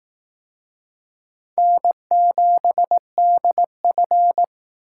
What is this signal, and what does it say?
Morse code, heard as a keyed tone, about 20 wpm, 700 Hz. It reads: N7DF